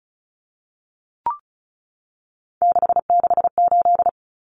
Morse code, heard as a keyed tone, about 34 wpm, 700 Hz. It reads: 668
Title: Numbers - 3-Digit 35wpm